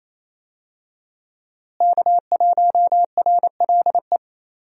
Morse code, heard as a keyed tone, about 28 wpm, 700 Hz. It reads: K1RLE
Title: USA Call Signs - 5 Characters - Repeated 3x - 28wpm